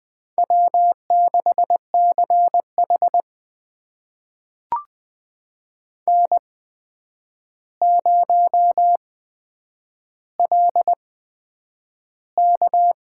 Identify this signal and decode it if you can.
Morse code, heard as a keyed tone, about 20 wpm, spaced as for 9 wpm, 700 Hz. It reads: W6CH N0LK
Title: USA Call Signs - 4 Characters - 8x Character Spacing - 20wpm